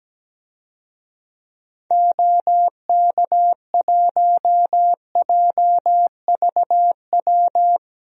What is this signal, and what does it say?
Morse code, heard as a keyed tone, about 17 wpm, 700 Hz. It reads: OK1JVW
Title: DXCC-100 Call Signs - 6 Characters - 17wpm